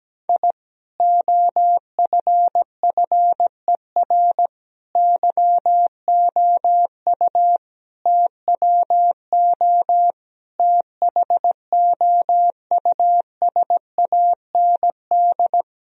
Morse code, heard as a keyed tone, about 17 wpm, 700 Hz. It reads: I OFFER YOU TWO THOUSAND
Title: Sentences from Top 1000 Words - Review Part 4 - 17wpm